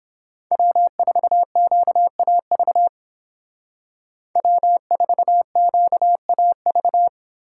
Morse code, heard as a keyed tone, about 29 wpm, 700 Hz. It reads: W4QAV W4QAV